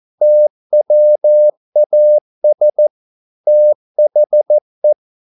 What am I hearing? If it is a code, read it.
Morse code, heard as a keyed tone, about 14 wpm, 600 Hz. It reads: TWAS THE
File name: mp3 files / a dogs mistake - 14wpm